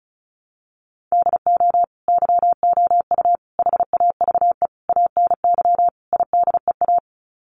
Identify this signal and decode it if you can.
Morse code, heard as a keyed tone, about 34 wpm, 700 Hz. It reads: DO YOU HAVE ANY IDEA
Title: Sets of 5 Words Encore 35wpm